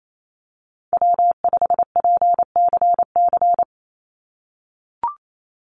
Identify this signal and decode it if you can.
Morse code, heard as a keyed tone, about 28 wpm, 700 Hz. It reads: W5PCC
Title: USA Call Signs 28wpm